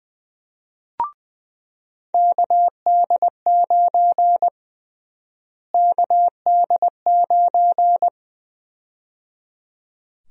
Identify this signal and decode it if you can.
Morse code, heard as a keyed tone, about 21 wpm, 700 Hz. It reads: KD9 KD9
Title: USA Call Sign Prefixes - Repeated 2x - 20wpm